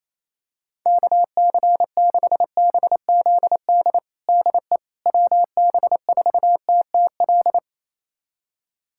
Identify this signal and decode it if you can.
Morse code, heard as a keyed tone, about 28 wpm, 700 Hz. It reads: KC6BZD DE WB4TTL
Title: USA Callsign DE Callsign - 6 Characters - 28wpm